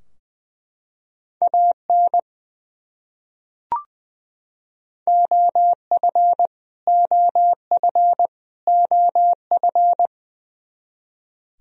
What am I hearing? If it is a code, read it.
Morse code, heard as a keyed tone, about 20 wpm, 700 Hz. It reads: AN OF OF OF
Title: Top 100 Words Repeated 20wpm